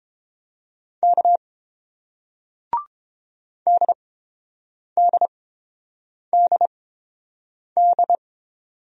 Morse code, heard as a keyed tone, about 31 wpm, spaced as for 7 wpm, 700 Hz. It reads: K DDDD